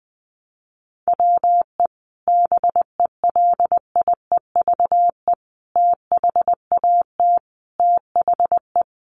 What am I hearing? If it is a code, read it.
Morse code, heard as a keyed tone, about 20 wpm, 700 Hz. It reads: WE BELIEVE THAT THE
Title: Sentences from Top 1300 Words 20wpm